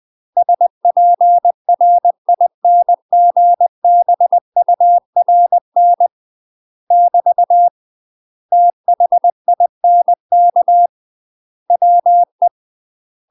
Morse code, heard as a keyed tone, about 20 wpm, 700 Hz. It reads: SPRINGBURN <BT> THINK WE
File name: UK MASTER QSOs TRAINING / 20WPM / UK MASTER QSOs 0004